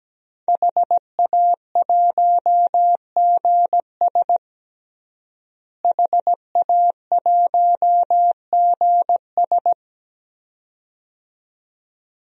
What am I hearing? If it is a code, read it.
Morse code, heard as a keyed tone, about 17 wpm, 700 Hz. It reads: HA1GS HA1GS
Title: DXCC-100 Call Signs - 5 Characters - Repeated 2x - 17wpm